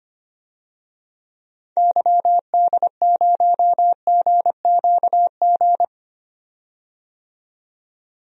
Morse code, heard as a keyed tone, about 25 wpm, 700 Hz. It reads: YD0GQG
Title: DXCC-100 Call Signs - 6 Characters - 25wpm